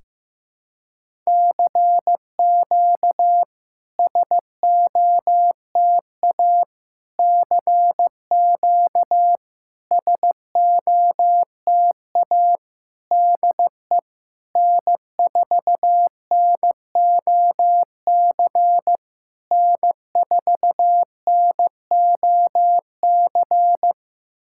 Morse code, heard as a keyed tone, about 15 wpm, 700 Hz. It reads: CQ SOTA CQ SOTA DE N4NOC N4NOC
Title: SOTA QSO Practice 15wpm